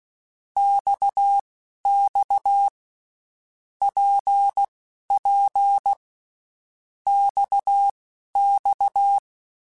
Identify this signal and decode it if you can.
Morse code, heard as a keyed tone, about 16 wpm, spaced as for 12 wpm, 800 Hz. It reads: XX PP XX
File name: Basic Letter Groups By7Groups / RKPX / RKPX@10